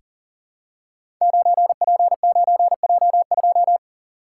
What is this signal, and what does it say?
Morse code, heard as a keyed tone, about 40 wpm, 700 Hz. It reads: 9P9J2
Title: Code Groups - 5 Letter:Numbers - 40wpm